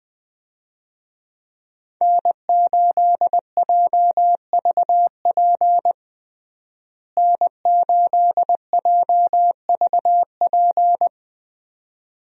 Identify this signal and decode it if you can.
Morse code, heard as a keyed tone, about 20 wpm, 700 Hz. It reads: N8JVP N8JVP
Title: USA Call Signs - 5 Characters - Repeated 2x - 20wpm